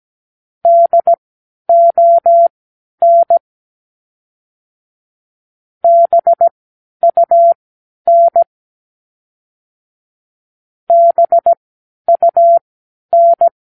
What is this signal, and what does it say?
Morse code, heard as a keyed tone, about 17 wpm, spaced as for 10 wpm, 680 Hz. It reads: DON BUN BUN
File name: k7qo / Practicewords